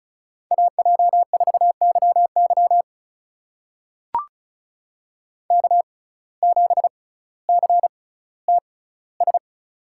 Morse code, heard as a keyed tone, about 36 wpm, spaced as for 18 wpm, 700 Hz. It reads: AJ4YY K7CTS
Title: USA Call Signs - 6x Character Spacing - 35wpm